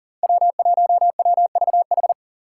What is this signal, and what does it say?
Morse code, heard as a keyed tone, about 40 wpm, 700 Hz. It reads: W1WVH
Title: Five Challenging Letter-Number Combinations 40wpm